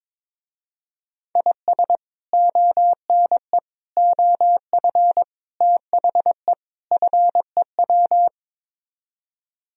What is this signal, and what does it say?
Morse code, heard as a keyed tone, about 22 wpm, 700 Hz. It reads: IS ONE OF THE FEW
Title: Speed-Racing - Sets of 5 Words 22wpm